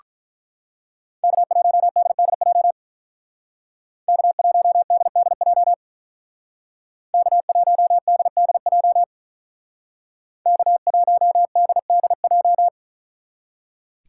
Morse code, heard as a keyed tone, about 41 wpm, 700 Hz. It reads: K1DDJ K1DDJ K1DDJ K1DDJ
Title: Speed Racing - USA 1x3 Call Signs 35wpm